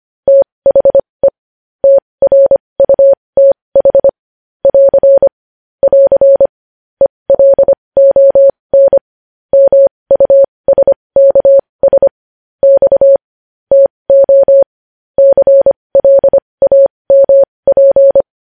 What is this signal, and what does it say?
Morse code, heard as a keyed tone, about 25 wpm, spaced as for 20 wpm, 550 Hz. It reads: THE TRUTH <AR> <AR> ELON MUSKS X TO CLAMP
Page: news headlines in morse code at 20 wpm